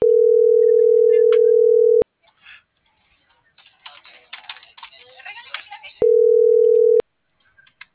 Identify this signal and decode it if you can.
Ringback tone (North America), 440 + 480 Hz.